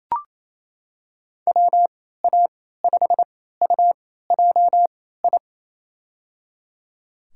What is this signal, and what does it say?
Morse code, heard as a keyed tone, about 27 wpm, spaced as for 16 wpm, 700 Hz. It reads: WA5UJI
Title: USA Call Signs - 3x Character Spacing - 28wpm